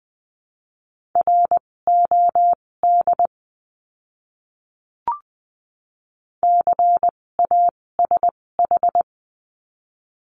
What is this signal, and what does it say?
Morse code, heard as a keyed tone, about 20 wpm, spaced as for 16 wpm, 700 Hz. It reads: ROD CASH
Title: 256 Lesson 42a - Words 12-20wpm